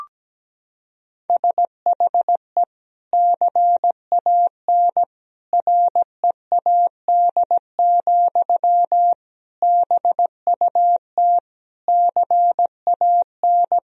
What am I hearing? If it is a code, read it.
Morse code, heard as a keyed tone, about 17 wpm, 700 Hz. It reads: SHE CAN READ, BUT CAN
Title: Sentences from Top 1000 Words - Review Part 1 - 17wpm